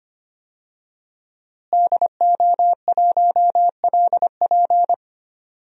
Morse code, heard as a keyed tone, about 25 wpm, 700 Hz. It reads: DO1LP
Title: DXCC-100 Call Signs - 5 Characters - Repeated 2x - 25wpm